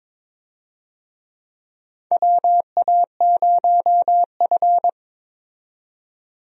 Morse code, heard as a keyed tone, about 22 wpm, 700 Hz. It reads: WA0F
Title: USA Call Signs - 4 Characters - 22wpm